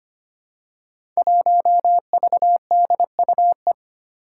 Morse code, heard as a keyed tone, about 25 wpm, 700 Hz. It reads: 1VDUE
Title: Five Challenging Letter-Number Combinations 25wpm